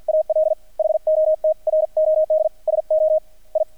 Morse code, heard as a keyed tone, about 50 wpm, 630 Hz. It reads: AL HOT ROD SO I
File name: archived / CRSnet